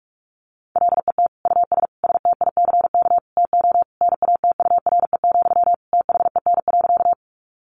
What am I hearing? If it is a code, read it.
Morse code, heard as a keyed tone, about 44 wpm, 700 Hz. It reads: LET US STICK TO NATURE, THEN.
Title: Sentences from Top 1200 Words 45wpm